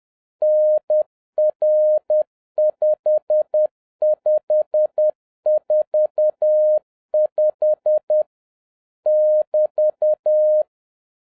Morse code, heard as a keyed tone, about 10 wpm, 610 Hz. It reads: NR5545 <BT>
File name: 2011-10-mp3 / n11-MEETING WITH GOOD FRIENDS 10wpm